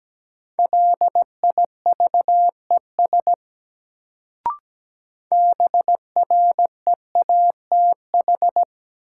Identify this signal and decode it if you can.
Morse code, heard as a keyed tone, about 17 wpm, 700 Hz. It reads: LIVES BREATH
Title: Sets of 5 Words Encore - New Word Drill - 17wpm